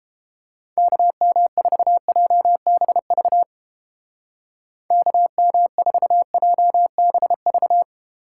Morse code, heard as a keyed tone, about 30 wpm, 700 Hz. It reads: KM4JBV KM4JBV